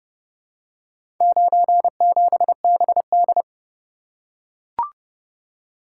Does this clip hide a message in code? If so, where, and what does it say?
Yes: Morse code from 1.2 to 3.4 s, heard as a keyed tone, about 29 wpm, 700 Hz. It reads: 97BD